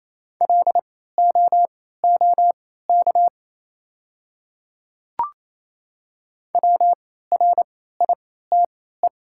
Morse code, heard as a keyed tone, about 28 wpm, spaced as for 16 wpm, 700 Hz. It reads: LOOK WRITE
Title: 3x Character Spacing - Top 200 Words - 28wpm